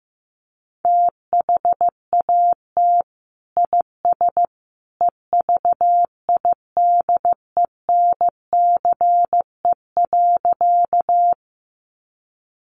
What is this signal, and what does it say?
Morse code, heard as a keyed tone, about 15 wpm, 700 Hz. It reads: THAT IS EVIDENCE.
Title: Sentences from Top 1300 Words 15wpm